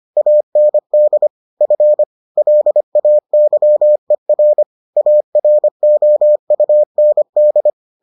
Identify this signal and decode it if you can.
Morse code, heard as a keyed tone, about 25 wpm, 600 Hz. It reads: AND F LAYER AROUND